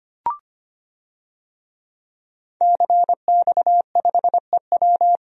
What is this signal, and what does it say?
Morse code, heard as a keyed tone, about 25 wpm, 700 Hz. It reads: CX5EW
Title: DXCC-100 Call Signs - 5 Characters - 25wpm